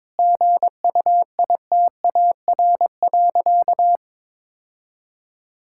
Morse code, heard as a keyed tone, about 22 wpm, 700 Hz. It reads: GUITAR.
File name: Difficult Sentences with Hints - 22wpm